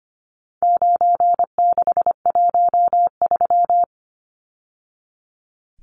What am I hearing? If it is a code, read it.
Morse code, heard as a keyed tone, about 25 wpm, 700 Hz. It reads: 9613
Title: Numbers - 4-Digit 25wpm